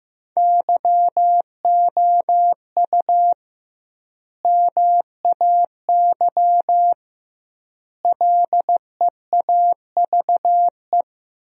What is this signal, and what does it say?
Morse code, heard as a keyed tone, about 16 wpm, 700 Hz. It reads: YOU MAY LEAVE
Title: 2x Word-Spacing - Sentences from Top 400 Words - 15wpm